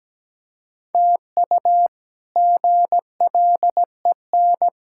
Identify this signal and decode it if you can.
Morse code, heard as a keyed tone, about 17 wpm, 700 Hz. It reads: TU GLEN
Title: K1USN SST QSO Practice - 17wpm